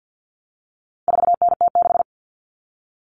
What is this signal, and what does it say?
Morse code, heard as a keyed tone, about 50 wpm, 700 Hz. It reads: 4NT6
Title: All Cut Numbers - 4-Digit 50wpm